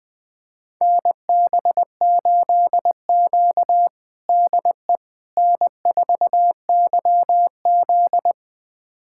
Morse code, heard as a keyed tone, about 21 wpm, 700 Hz. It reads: NB8Q DE N4YZ
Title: USA Callsign DE Callsign - 4 Characters - 20wpm